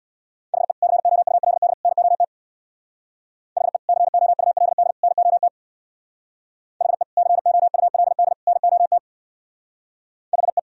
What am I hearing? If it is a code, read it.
Morse code, heard as a keyed tone, about 58 wpm, 700 Hz. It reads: HE COULD NOT HE COULD NOT HE COULD NOT HE
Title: Speed-Racing - Sets of 3 Words 50wpm